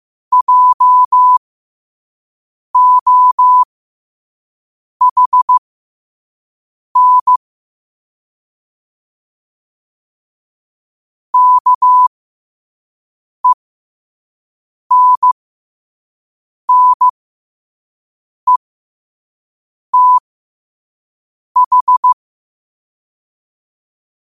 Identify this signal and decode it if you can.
Morse code, heard as a keyed tone, about 15 wpm, spaced as for 5 wpm, 1000 Hz. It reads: JOHN KENNETH
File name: 2026-03-16-5wpm